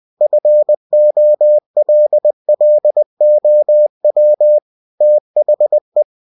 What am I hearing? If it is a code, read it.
Morse code, heard as a keyed tone, about 20 wpm, 600 Hz. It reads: FOLLOW THE
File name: cw-books / winnie-the-pooh / winnie-20wpm-page-0113